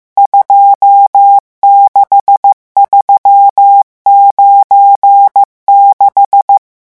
Morse code, heard as a keyed tone, about 15 wpm, 800 Hz. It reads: 26396
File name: morse mp3 / revision11 chiffres vitesse 14